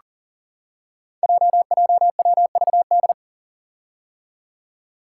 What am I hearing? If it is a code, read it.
Morse code, heard as a keyed tone, about 40 wpm, 700 Hz. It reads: JJWVD
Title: Code Groups - 5 Letters - 40wpm